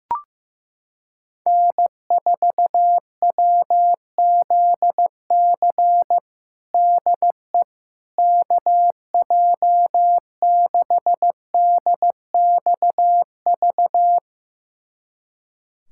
Morse code, heard as a keyed tone, about 15 wpm, 700 Hz. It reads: N4WZC DE KJ6DXV
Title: USA Callsign DE Callsign 15wpm